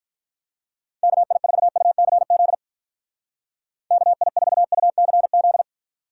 Morse code, heard as a keyed tone, about 45 wpm, 700 Hz. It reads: KI4UCZ KI4UCZ